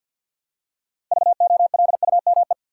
Morse code, heard as a keyed tone, about 50 wpm, 700 Hz. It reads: VOLUME
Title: Six Letter Words - 50wpm